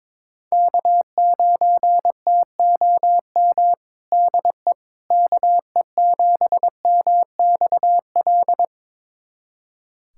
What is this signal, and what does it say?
Morse code, heard as a keyed tone, about 22 wpm, 700 Hz. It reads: K9TOM DE KE7MXL
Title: USA Callsign DE Callsign 22wpm